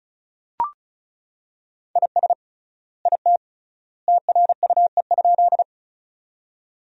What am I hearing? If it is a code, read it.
Morse code, heard as a keyed tone, about 36 wpm, 700 Hz. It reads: IS IT TRUE?
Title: 3x Word-Spacing - Sentences from Top 400 Words - 35wpm